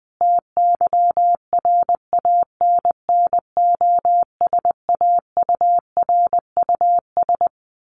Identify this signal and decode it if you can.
Morse code, heard as a keyed tone, about 20 wpm, 700 Hz. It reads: TYRANNOSAURUS
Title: A Sound Of Thunder 20wpm